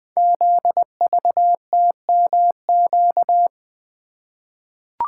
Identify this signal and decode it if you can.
Morse code, heard as a keyed tone, about 20 wpm, 700 Hz. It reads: ZVTMQ
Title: Code Groups - 5 Letters - 20wpm